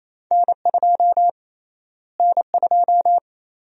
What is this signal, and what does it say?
Morse code, heard as a keyed tone, about 28 wpm, 700 Hz. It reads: N2 N2
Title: USA Call Sign Prefixes - Repeated 2x - 28wpm